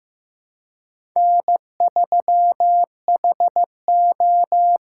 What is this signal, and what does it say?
Morse code, heard as a keyed tone, about 18 wpm, 700 Hz. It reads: N3HO
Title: Speed Racing - USA 1x2 Call Signs 15wpm